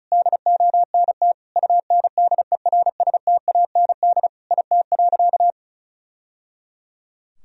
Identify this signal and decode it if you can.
Morse code, heard as a keyed tone, about 36 wpm, 700 Hz. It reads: DONT UNDERSTAND IT.
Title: Sentences from Top 1000 Words - Review Part 4 - 35wpm